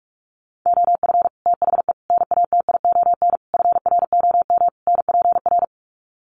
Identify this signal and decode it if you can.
Morse code, heard as a keyed tone, about 44 wpm, 700 Hz. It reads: OF THE NATION FROM NPR
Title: Sets of 5 Words 45wpm